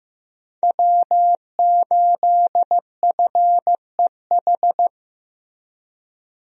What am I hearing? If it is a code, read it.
Morse code, heard as a keyed tone, about 15 wpm, 700 Hz. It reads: W8FEH
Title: USA Call Signs - 5 Characters - Repeated 2x - 15wpm